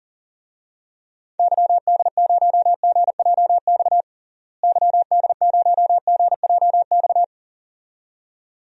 Morse code, heard as a keyed tone, about 40 wpm, 700 Hz. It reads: YD0GJX YD0GJX
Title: DXCC-100 Call Signs - 6 Characters - Repeated 2x - 40wpm